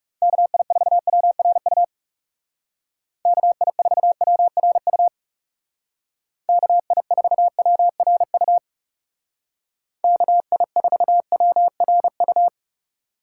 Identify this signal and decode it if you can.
Morse code, heard as a keyed tone, about 38 wpm, 700 Hz. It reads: KI4WRU KI4WRU KI4WRU KI4WRU